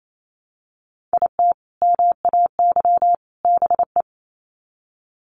Morse code, heard as a keyed tone, about 28 wpm, 700 Hz. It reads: IT MAY BE
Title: Sets of 3 Words 28wpm